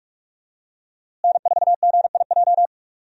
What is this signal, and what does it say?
Morse code, heard as a keyed tone, about 46 wpm, 700 Hz. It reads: N4GIJ
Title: USA Call Signs - 5 Characters - Repeated 2x - 45wpm